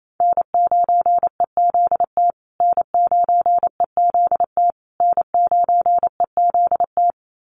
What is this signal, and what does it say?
Morse code, heard as a keyed tone, about 28 wpm, 700 Hz. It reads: N9EZT N9EZT N9EZT
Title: USA Call Signs Repeated 3x 28wpm